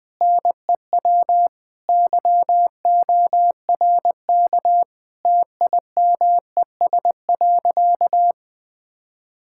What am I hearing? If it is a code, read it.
Morse code, heard as a keyed tone, about 20 wpm, 700 Hz. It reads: NEW YORK TIMES.
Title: Sentences with 4-Word Phrases - 20wpm